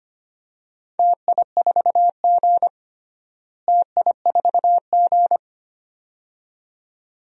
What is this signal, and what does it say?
Morse code, heard as a keyed tone, about 25 wpm, 700 Hz. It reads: TI4G TI4G
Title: DXCC-100 Call Signs - 4 Characters - Repeated 2x - 25wpm